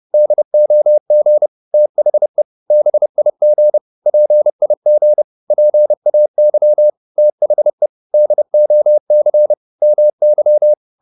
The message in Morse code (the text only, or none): DOG THE BIG PIG PAY THE DOC MY